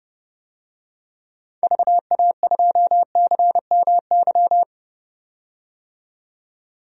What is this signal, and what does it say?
Morse code, heard as a keyed tone, about 32 wpm, 700 Hz. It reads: VA2CMY
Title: DXCC-100 Call Signs - 6 Characters - 30wpm